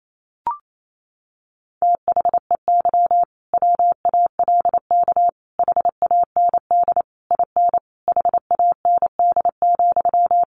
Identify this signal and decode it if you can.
Morse code, heard as a keyed tone, about 28 wpm, 700 Hz. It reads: THEY WALK HAND IN HAND,
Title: Sentences from Top 1400 Words 28wpm